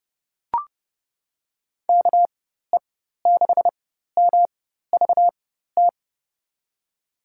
Morse code, heard as a keyed tone, about 30 wpm, spaced as for 14 wpm, 700 Hz. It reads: KE6MVT
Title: USA Call Signs - 6 Characters - 4x Character Spacing - 30wpm